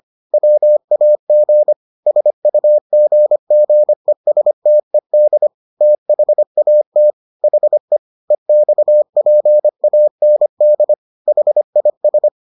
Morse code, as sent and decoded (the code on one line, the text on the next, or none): .-- .- --. / ... ..- --. --. ... - . -.. / - .... .- - / .... / . -..- .--. .- -. -.. / .... .. ...
WAG SUGGESTED THAT HE EXPAND HIS